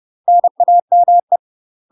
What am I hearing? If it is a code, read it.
Morse code, heard as a keyed tone, about 30 wpm, 700 Hz. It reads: NAME
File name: UK MASTER QSOs TRAINING / 30WPM / UK MASTER QSOs 0049